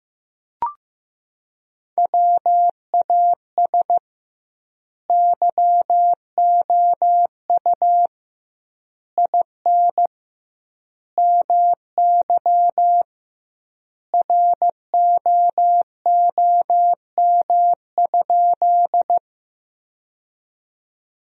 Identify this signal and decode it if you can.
Morse code, heard as a keyed tone, about 16 wpm, 700 Hz. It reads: WAS YOU IN MY ROOM?
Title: 2x Word-Spacing - Sentences from Top 300 Words - 15wpm